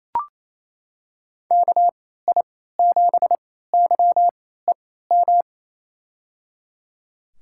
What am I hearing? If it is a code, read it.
Morse code, heard as a keyed tone, about 28 wpm, spaced as for 16 wpm, 700 Hz. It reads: KI7YEM
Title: USA Call Signs - 6 Characters - 3x Character Spacing - 28wpm